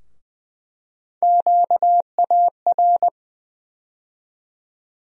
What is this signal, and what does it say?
Morse code, heard as a keyed tone, about 20 wpm, 700 Hz. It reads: QAR
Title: USA Call Sign Postfixes - 3 Letters 20wpm